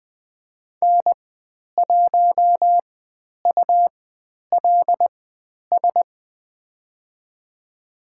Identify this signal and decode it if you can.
Morse code, heard as a keyed tone, about 20 wpm, spaced as for 10 wpm, 700 Hz. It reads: N1ULS
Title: 102 Lesson 17g - Letter u - Call Signs 20wpm